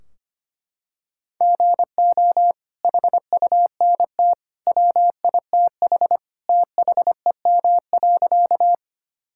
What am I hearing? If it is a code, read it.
Morse code, heard as a keyed tone, about 25 wpm, 700 Hz. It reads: GO HUNT WITH THEM.
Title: Sentences from Top 1600 Words 25wpm